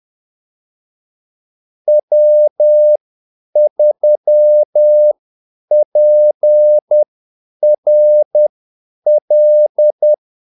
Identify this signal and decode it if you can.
Morse code, heard as a keyed tone, about 10 wpm, spaced as for 8 wpm, 600 Hz. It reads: W3PRL
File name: Callsigns-10-8wpm